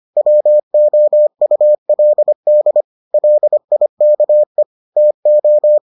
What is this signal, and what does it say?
Morse code, heard as a keyed tone, about 25 wpm, 600 Hz. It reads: WOULD LIKE TO